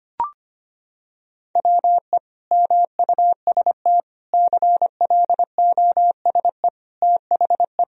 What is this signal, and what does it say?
Morse code, heard as a keyed tone, about 25 wpm, 700 Hz. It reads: WE MUST CLOSE THE
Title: Sentences from Top 1000 Words - Review Part 1 - 25wpm